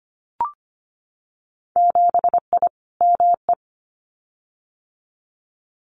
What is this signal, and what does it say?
Morse code, heard as a keyed tone, about 25 wpm, 700 Hz. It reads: 7I ME
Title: Winter Field Day Exchanges Encore 25wpm